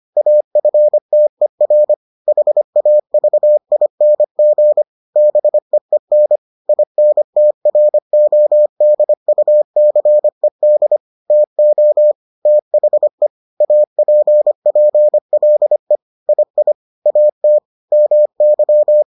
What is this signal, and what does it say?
Morse code, heard as a keyed tone, about 25 wpm, 600 Hz. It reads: AFTER HAVING BEEN INTRODUCED TO THE APPLE II AT MY